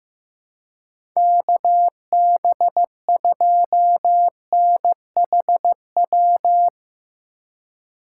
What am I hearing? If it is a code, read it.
Morse code, heard as a keyed tone, about 15 wpm, 700 Hz. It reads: KB2NHW
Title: USA Call Signs - 6 Characters - 2x Character Spacing - 15wpm